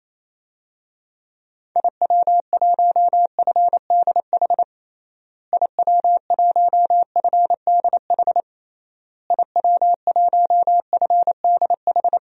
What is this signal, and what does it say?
Morse code, heard as a keyed tone, about 28 wpm, 700 Hz. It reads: IW1FDH IW1FDH IW1FDH
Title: DXCC-100 Call Signs - 6 Characters - Repeated 3x - 28wpm